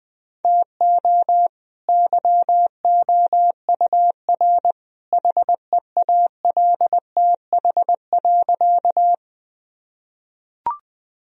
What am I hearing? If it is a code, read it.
Morse code, heard as a keyed tone, about 20 wpm, 700 Hz. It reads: TO YOUR HEALTH.
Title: Sentences from Top 1000 Words - Review Part 3 - 20wpm